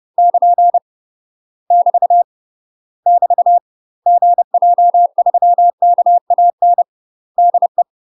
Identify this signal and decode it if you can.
Morse code, heard as a keyed tone, about 30 wpm, 700 Hz. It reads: <KN> <BT> <BT> GJ3KAN DE
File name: UK MASTER QSOs TRAINING/30WPM